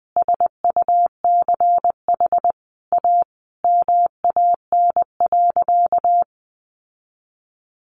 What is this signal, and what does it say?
Morse code, heard as a keyed tone, about 20 wpm, 700 Hz. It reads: SUCH A MAN.
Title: Sentences from Top 1300 Words 20wpm